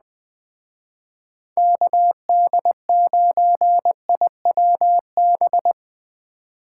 Morse code, heard as a keyed tone, about 20 wpm, 700 Hz. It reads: KD9IWB
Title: Speed Racing - USA Call Signs 17wpm